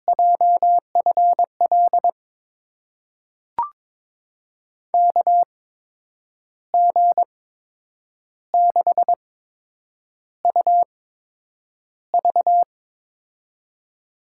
Morse code, heard as a keyed tone, about 22 wpm, spaced as for 8 wpm, 700 Hz. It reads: JFL KG6UV
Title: USA Call Signs - 5 Characters - 8x Character Spacing - 22wpm